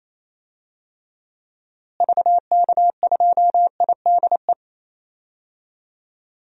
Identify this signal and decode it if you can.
Morse code, heard as a keyed tone, about 27 wpm, 700 Hz. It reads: VK2IDE
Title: DXCC-100 Call Signs - 6 Characters - Repeated 2x - 28wpm